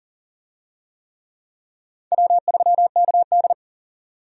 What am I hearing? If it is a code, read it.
Morse code, heard as a keyed tone, about 40 wpm, 700 Hz. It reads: W3KD
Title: USA Call Signs - 4 Characters - 40wpm